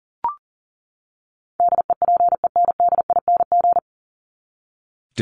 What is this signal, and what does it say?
Morse code, heard as a keyed tone, about 40 wpm, 700 Hz. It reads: DEPENDING
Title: ING Suffix - Top 100 Words with Base Words 40wpm